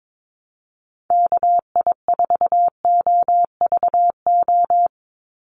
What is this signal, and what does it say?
Morse code, heard as a keyed tone, about 22 wpm, 700 Hz. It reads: KI4OVO
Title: USA Call Signs 22wpm